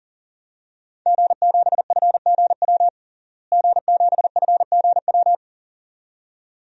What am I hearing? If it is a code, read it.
Morse code, heard as a keyed tone, about 37 wpm, 700 Hz. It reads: G7FGW G7FGW
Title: DXCC-100 Call Signs - 5 Characters - Repeated 2x - 40wpm